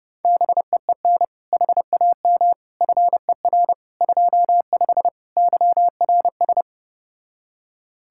Morse code, heard as a keyed tone, about 33 wpm, 700 Hz. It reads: BEEN HAM FER 25 YRS